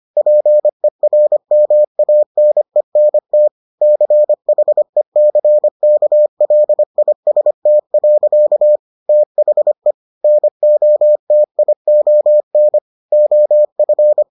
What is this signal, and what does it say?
Morse code, heard as a keyed tone, about 25 wpm, 600 Hz. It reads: PERMANENT CHECKLIST. THE NOTION OF